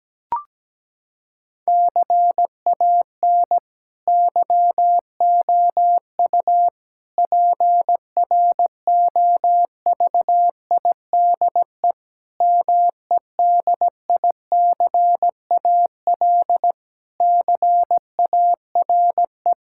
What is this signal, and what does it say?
Morse code, heard as a keyed tone, about 17 wpm, 700 Hz. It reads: CAN YOU PROVIDE MEDICAL CARE
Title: Sentences from Top 1000 Words - Review Part 5 - 17wpm